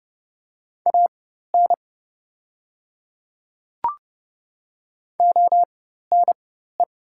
Morse code, heard as a keyed tone, about 30 wpm, spaced as for 14 wpm, 700 Hz. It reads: AN ONE
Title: 4x Character Spacing - Top 100 Words - 30wpm